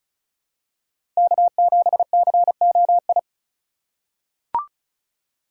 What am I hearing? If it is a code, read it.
Morse code, heard as a keyed tone, about 36 wpm, 700 Hz. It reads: K7COI
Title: USA Call Signs - 5 Characters - 4x Character Spacing - 35wpm